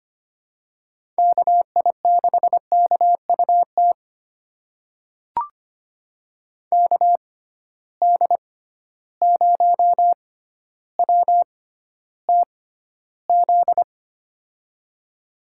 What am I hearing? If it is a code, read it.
Morse code, heard as a keyed tone, about 25 wpm, spaced as for 13 wpm, 700 Hz. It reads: KI6KUT KD0WTZ
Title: USA Call Signs - 6 Characters - 6x Character Spacing - 25wpm